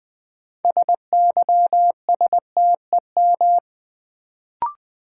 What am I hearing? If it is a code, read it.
Morse code, heard as a keyed tone, about 20 wpm, 700 Hz. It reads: SYSTEM
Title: Sentences from Top 1000 Words - Review Part 3 - New Word Drill - 20wpm